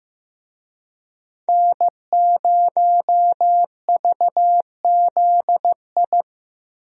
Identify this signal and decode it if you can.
Morse code, heard as a keyed tone, about 15 wpm, 700 Hz. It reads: N0VZI
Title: USA Call Signs - 5 Characters - Repeated 2x - 15wpm